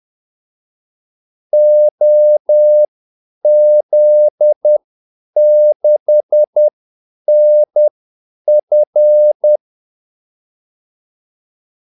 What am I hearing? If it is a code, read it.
Morse code, heard as a keyed tone, about 10 wpm, spaced as for 8 wpm, 600 Hz. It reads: OZ6NF
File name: Callsigns-10-8wpm